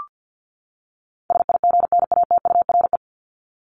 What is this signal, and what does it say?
Morse code, heard as a keyed tone, about 50 wpm, 700 Hz. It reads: SIGNATURE
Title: Top 1751-2000 Words 50wpm